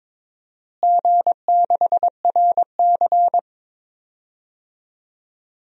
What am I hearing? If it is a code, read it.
Morse code, heard as a keyed tone, about 22 wpm, 700 Hz. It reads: G6RC
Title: DXCC-100 Call Signs - 4 Characters - Repeated 2x - 22wpm